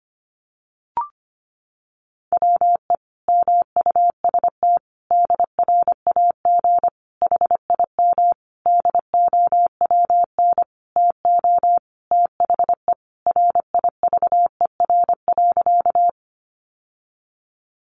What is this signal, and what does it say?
Morse code, heard as a keyed tone, about 25 wpm, 700 Hz. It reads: WE MUST DRAG HIM DOWN TO THE RIVER.
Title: Sentences from Top 1500 Words 25wpm